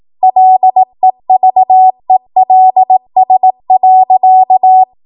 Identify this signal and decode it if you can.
Morse code, heard as a keyed tone, about 18 wpm, 750 Hz. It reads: LEVELS.
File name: Morse / 18WPM